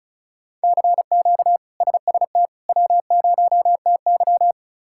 Morse code, heard as a keyed tone, about 36 wpm, 700 Hz. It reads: CQ SST W0TY
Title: K1USN SST QSO Practice - 35wpm